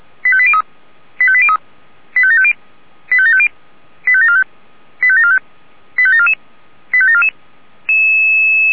Five-tone selective call (ZVEI), heard as quick runs of five tones, tones of about 70 ms. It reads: 85803 85803 85750 85750 85745 85745 85744 85744
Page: /464/SDR/ZVEI/85726/2023.07.22/